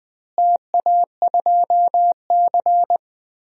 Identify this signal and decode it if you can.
Morse code, heard as a keyed tone, about 21 wpm, 700 Hz. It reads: TA2C
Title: DXCC-100 Call Signs - 4 Characters - Repeated 3x - 20wpm